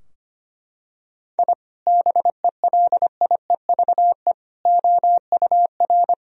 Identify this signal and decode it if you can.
Morse code, heard as a keyed tone, about 25 wpm, 700 Hz. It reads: I BELIEVE OUR
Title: Sentences from Top 1000 Words - Review Part 3 - 25wpm